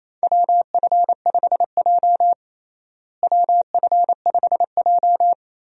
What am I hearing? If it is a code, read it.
Morse code, heard as a keyed tone, about 28 wpm, 700 Hz. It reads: WF5J WF5J
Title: USA Call Signs - 4 Characters - Repeated 2x - 28wpm